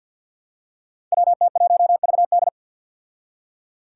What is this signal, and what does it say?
Morse code, heard as a keyed tone, about 48 wpm, 700 Hz. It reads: WT1VD
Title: Five Challenging Letter-Number Combinations 50wpm